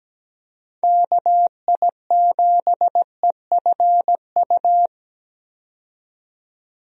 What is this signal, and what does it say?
Morse code, heard as a keyed tone, about 17 wpm, 700 Hz. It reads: KI7EFU
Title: USA Call Signs - 6 Characters - 17wpm